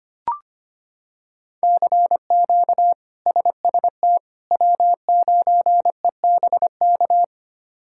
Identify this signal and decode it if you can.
Morse code, heard as a keyed tone, about 25 wpm, 700 Hz. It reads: CQ SST W9EBK
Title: K1USN SST QSO Practice - 25wpm